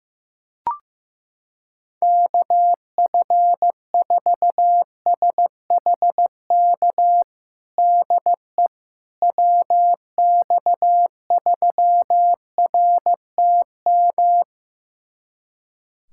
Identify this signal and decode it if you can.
Morse code, heard as a keyed tone, about 15 wpm, 700 Hz. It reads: KF4SHK DE WX3RTM
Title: USA Callsign DE Callsign - 6 Characters - 15wpm